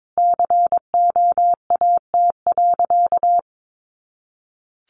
Morse code, heard as a keyed tone, about 22 wpm, 700 Hz. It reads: COAT.
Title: Additional Easy Phrases 22wpm